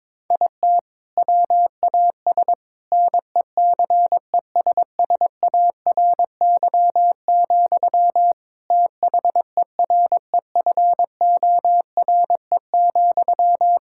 Morse code, heard as a keyed tone, about 22 wpm, 700 Hz. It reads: IT WAS NECESSARY, THEREFORE,
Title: Sentences from Top 1600 Words 22wpm